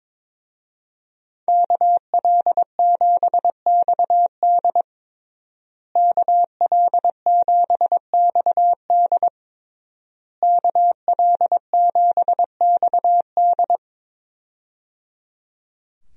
Morse code, heard as a keyed tone, about 22 wpm, 700 Hz. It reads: KL7XD KL7XD KL7XD
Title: DXCC-100 Call Signs - 5 Characters - Repeated 3x - 22wpm